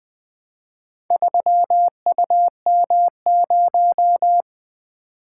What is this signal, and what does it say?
Morse code, heard as a keyed tone, about 20 wpm, 700 Hz. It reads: 3UM0
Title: Four Letter-Number Combinations - 20wpm